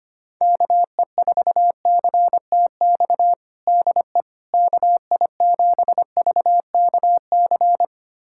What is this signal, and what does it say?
Morse code, heard as a keyed tone, about 25 wpm, 700 Hz. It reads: KE4CTX DE KI7VKC
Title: USA Callsign DE Callsign - 6 Characters - 25wpm